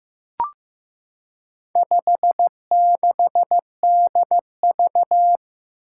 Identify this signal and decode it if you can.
Morse code, heard as a keyed tone, about 15 wpm, 700 Hz. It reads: 56DV
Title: Four Challenging Letter-Number Combinations 15wpm